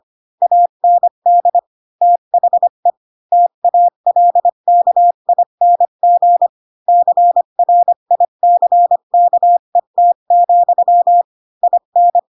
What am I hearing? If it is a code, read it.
Morse code, heard as a keyed tone, about 25 wpm, 700 Hz. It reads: AND THE TALKING CRICKET, IN
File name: Pinocchio / Pinocchio ch4 20 25 ik1hge